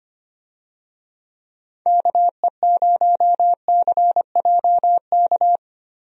Morse code, heard as a keyed tone, about 25 wpm, 700 Hz. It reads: KE0CJK
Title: USA Call Signs - 6 Characters - Repeated 2x - 25wpm